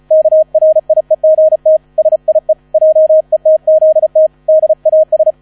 Morse code, heard as a keyed tone, about 34 wpm, 630 Hz, with radio noise. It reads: KRIEGT SIE JETZT DAS
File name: 35wpm